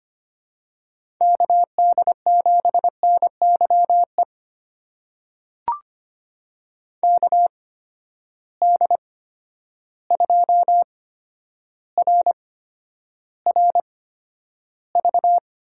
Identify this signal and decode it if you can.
Morse code, heard as a keyed tone, about 25 wpm, spaced as for 11 wpm, 700 Hz. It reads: KD7NYE KD2RRV
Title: USA Call Signs - 6 Characters - 8x Character Spacing - 25wpm